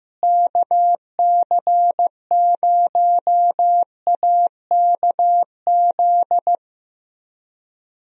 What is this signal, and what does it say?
Morse code, heard as a keyed tone, about 16 wpm, 700 Hz. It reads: KC0AKZ
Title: USA Call Signs - 6 Characters - Repeated 2x - 15wpm